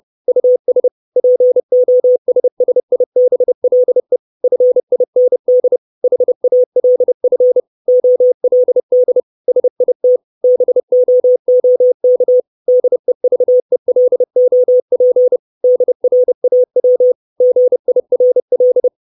Morse code, heard as a keyed tone, about 30 wpm, 500 Hz. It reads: US POSSIBLE FIND HALF OLD SIT BOOK DEVELOP DRAW GIRL
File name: 632 Words 30WPM